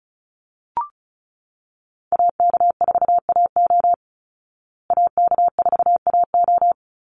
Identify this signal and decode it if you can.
Morse code, heard as a keyed tone, about 34 wpm, 700 Hz. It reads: AK4AO AK4AO
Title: Call Signs Repeated 3x - 35wpm